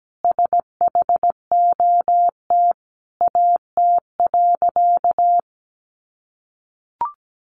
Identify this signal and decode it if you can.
Morse code, heard as a keyed tone, about 17 wpm, 700 Hz. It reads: SHOT AT.
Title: Sentences from Top 1100 Words 17wpm